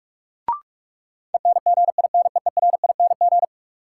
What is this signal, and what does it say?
Morse code, heard as a keyed tone, about 43 wpm, 700 Hz. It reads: ENGINEERING
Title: Sentences from Top 1500 Words - New Word Drill Alt - 45wpm